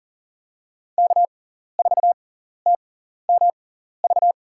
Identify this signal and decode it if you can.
Morse code, heard as a keyed tone, about 40 wpm, spaced as for 14 wpm, 700 Hz. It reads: K4TMV
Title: USA Call Signs - 6x Character Spacing - 40wpm